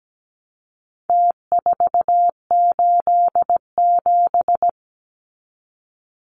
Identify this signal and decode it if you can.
Morse code, heard as a keyed tone, about 17 wpm, 700 Hz. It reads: T487
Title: Numbers - Contest Style - 4-Digit 17wpm